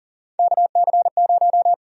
Morse code, heard as a keyed tone, about 40 wpm, 700 Hz. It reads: KC0
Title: USA Call Sign Prefixes - Repeated 2x - 40wpm